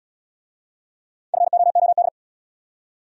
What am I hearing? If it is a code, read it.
Morse code, heard as a keyed tone, about 78 wpm, 700 Hz. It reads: FOOD